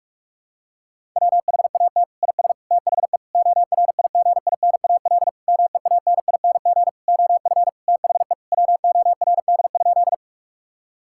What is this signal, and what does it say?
Morse code, heard as a keyed tone, about 46 wpm, 700 Hz. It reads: WHAT IS THE ORIGINAL MEANING OF THE WORD?
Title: Sentences from Top 1000 Words - Review Part 6 - 45wpm